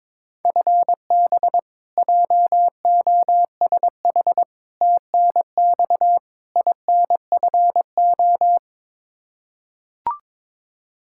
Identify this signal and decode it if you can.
Morse code, heard as a keyed tone, about 22 wpm, 700 Hz. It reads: FB JOSH TNX INFO
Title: Practice QSO - Exchange 3 - More Info - 22wpm